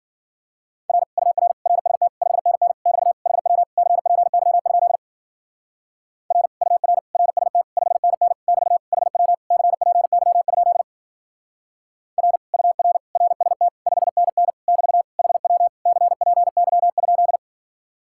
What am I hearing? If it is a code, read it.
Morse code, heard as a keyed tone, about 47 wpm, 700 Hz. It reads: R UR RST 5NN <BT> HW CPY? R UR RST 5NN <BT> HW CPY? R UR RST 5NN <BT> HW CPY?
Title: Speed-Racing Common QSO Phrases Encore - 40wpm